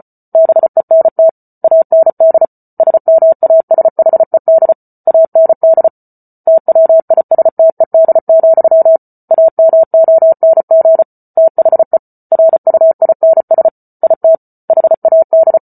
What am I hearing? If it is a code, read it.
Morse code, heard as a keyed tone, about 34 wpm, 650 Hz. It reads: BENT AND SMASHED AND TWISTED, AMONG THE RUINS IT HAD